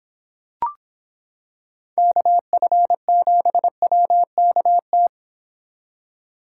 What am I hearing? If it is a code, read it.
Morse code, heard as a keyed tone, about 26 wpm, 700 Hz. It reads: KF7WKT